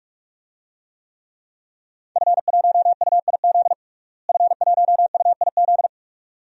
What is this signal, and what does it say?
Morse code, heard as a keyed tone, about 46 wpm, 700 Hz. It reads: F1UIZ F1UIZ
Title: DXCC-100 Call Signs - 5 Characters - Repeated 2x - 45wpm